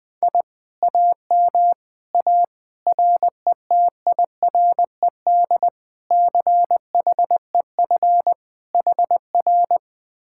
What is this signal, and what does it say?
Morse code, heard as a keyed tone, about 20 wpm, 700 Hz. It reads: I AM A RETIRED CHEF HR
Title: Speed-Racing Common QSO Phrases - 17wpm